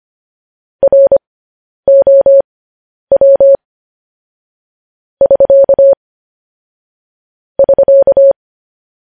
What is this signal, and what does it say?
Morse code, heard as a keyed tone, about 25 wpm, spaced as for 10 wpm, 550 Hz. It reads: ROW <SK> <SK>
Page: news-headlines-in-morse-code-at-10-wpm